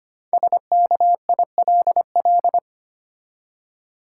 Morse code, heard as a keyed tone, about 25 wpm, 700 Hz. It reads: SKILL
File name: 5 Letter Words - First 3 Letters Given - 25wpm